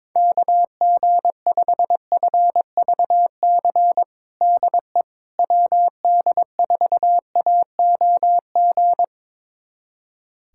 Morse code, heard as a keyed tone, about 22 wpm, 700 Hz. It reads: KG5FVC DE WD4AOG
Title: USA Callsign DE Callsign 22wpm